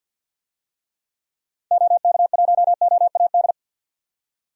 Morse code, heard as a keyed tone, about 46 wpm, 700 Hz. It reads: OK1OAD
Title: DXCC-100 Call Signs - 6 Characters - 50wpm